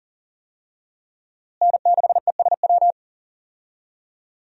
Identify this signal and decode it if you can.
Morse code, heard as a keyed tone, about 40 wpm, 700 Hz. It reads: N6ESW
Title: USA Call Signs - 5 Characters - Repeated 2x - 40wpm